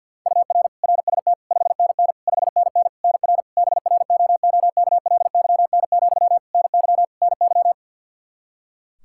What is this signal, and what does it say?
Morse code, heard as a keyed tone, about 49 wpm, 700 Hz. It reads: UR RST 5NN 5NN NR BROOKLYN, NY NY
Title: K1USN SST QSO Practice 50wpm